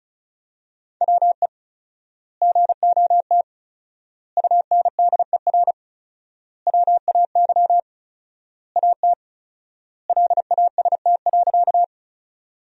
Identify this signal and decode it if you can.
Morse code, heard as a keyed tone, about 36 wpm, 700 Hz. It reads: WE GOT UNDER WAY AT LAST.
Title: 4x Word-Spacing - Sentences from Top 300 Words - 35wpm